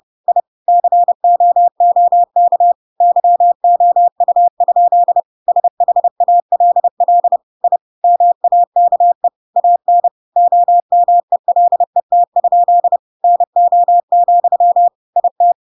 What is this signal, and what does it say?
Morse code, heard as a keyed tone, about 30 wpm, 700 Hz. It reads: I COOK YOU? SHALL I MAKE AN OMELET? NO, IT
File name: Pinocchio / Pinocchio ch5 30 30 ik1hge